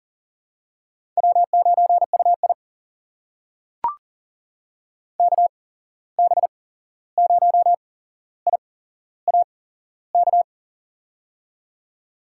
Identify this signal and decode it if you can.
Morse code, heard as a keyed tone, about 39 wpm, spaced as for 15 wpm, 700 Hz. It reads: W9UI KB0IAK
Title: USA Call Signs - 8x Character Spacing - 40wpm